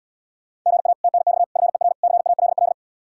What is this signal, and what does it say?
Morse code, heard as a keyed tone, about 78 wpm, 700 Hz. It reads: GA TNX FER CALL